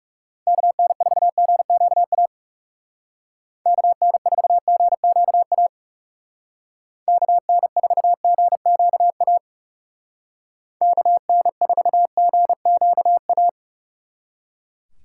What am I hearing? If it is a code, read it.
Morse code, heard as a keyed tone, about 38 wpm, 700 Hz. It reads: KN4GQA KN4GQA KN4GQA KN4GQA